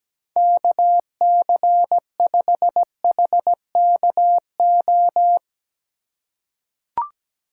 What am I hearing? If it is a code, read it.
Morse code, heard as a keyed tone, about 17 wpm, 700 Hz. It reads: KC5HKO
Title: USA Call Signs - 2x Character Spacing - 17wpm